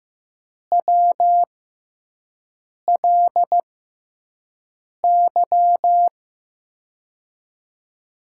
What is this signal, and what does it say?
Morse code, heard as a keyed tone, about 15 wpm, spaced as for 5 wpm, 700 Hz. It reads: WLY
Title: USA Call Signs - 6x Character Spacing - 15wpm